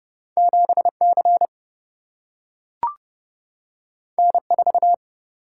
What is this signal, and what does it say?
Morse code, heard as a keyed tone, about 30 wpm, 700 Hz. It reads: 7 C N 4
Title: Single-Letter Single-Number 30wpm